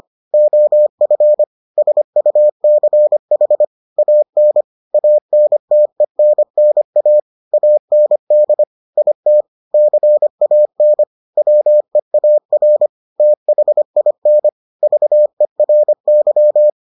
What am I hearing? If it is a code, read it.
Morse code, heard as a keyed tone, about 25 wpm, 600 Hz. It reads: OF SUCH AN ANTENNA AND IT CAN WEAR THIN VERY